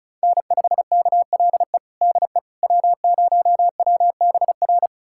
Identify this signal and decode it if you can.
Morse code, heard as a keyed tone, about 36 wpm, 700 Hz. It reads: N5KLE DE W0WBR
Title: USA Callsign DE Callsign - 5 Characters - 35wpm